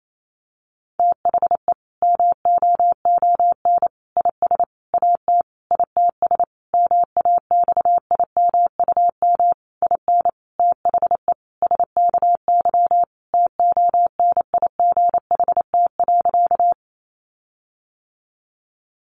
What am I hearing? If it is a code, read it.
Morse code, heard as a keyed tone, about 28 wpm, 700 Hz. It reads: THE MOON IS AT ITS MAXIMUM IN THE SKY TONIGHT.
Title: Sentences from Top 1300 Words 28wpm